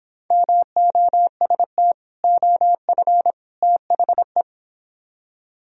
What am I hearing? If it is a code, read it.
Morse code, heard as a keyed tone, about 26 wpm, 700 Hz. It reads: MOST OF THE